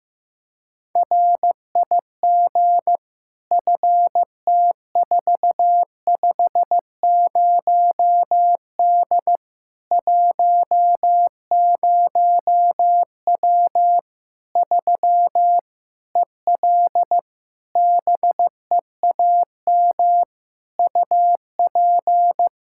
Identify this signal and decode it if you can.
Morse code, heard as a keyed tone, about 15 wpm, 700 Hz. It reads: RIG FT450D 10W 3 EL BEAM UP
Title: Practice QSO - Exchange 2 - Info - 15wpm